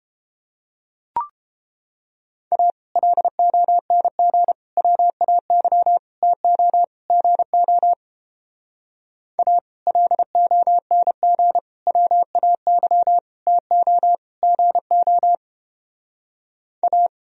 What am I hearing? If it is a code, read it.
Morse code, heard as a keyed tone, about 29 wpm, 700 Hz. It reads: A LONG WAY TO GO A LONG WAY TO GO A